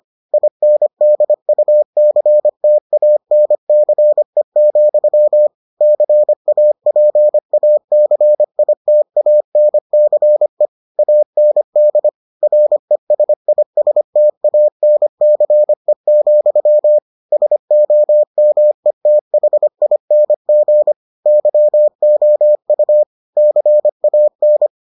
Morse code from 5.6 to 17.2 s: CAPACITANCE AND RESISTANCE,

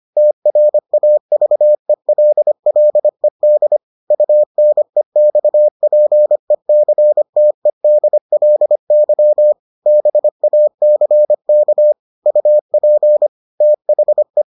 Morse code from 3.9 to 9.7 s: UNEXPECTEDLY

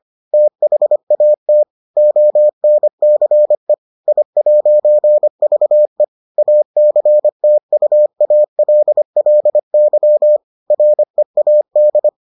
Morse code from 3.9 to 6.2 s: I'VE

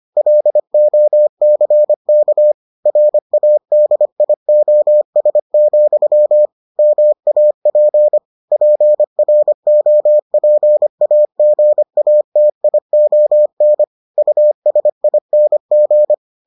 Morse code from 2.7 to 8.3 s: RADIOS, MAP

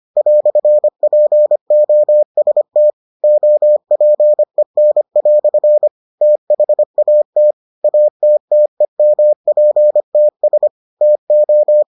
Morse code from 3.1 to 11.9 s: OPEN" THAT ATTEMPTS TO

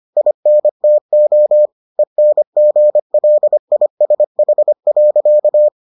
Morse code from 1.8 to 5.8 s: ENGLISH.

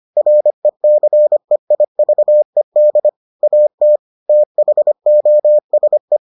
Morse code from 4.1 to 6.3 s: THOSE

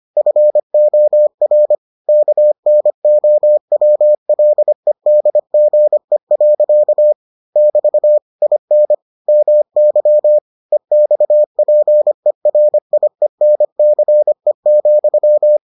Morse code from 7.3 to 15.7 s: <BT> IN MY EXPERIENCE,

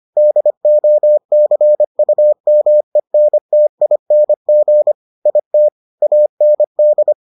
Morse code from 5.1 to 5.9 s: IT

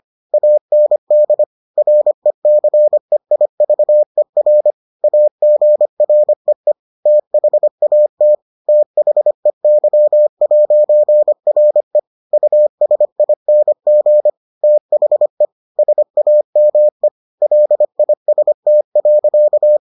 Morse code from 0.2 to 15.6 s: AND RECEIVER AGREE THAT THEY'RE USING THE

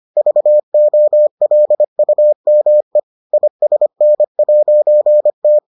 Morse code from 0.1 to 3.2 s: VOLUME

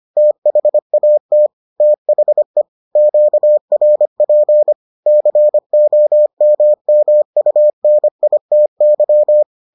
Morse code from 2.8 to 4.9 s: QRP

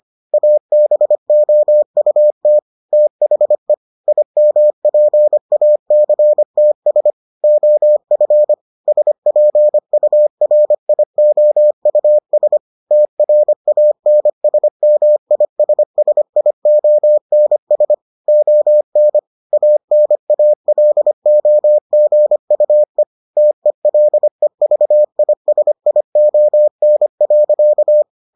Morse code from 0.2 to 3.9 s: ABOUT THE

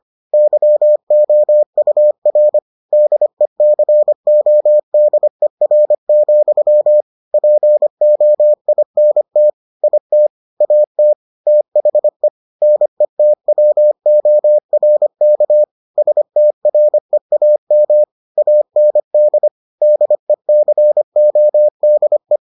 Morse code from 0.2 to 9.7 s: YOUR DECODER, POINT

